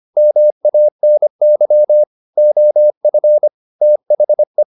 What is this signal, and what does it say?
Morse code, heard as a keyed tone, about 25 wpm, 600 Hz. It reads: MANY OF THE
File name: the old man and the sea part 1 25wpm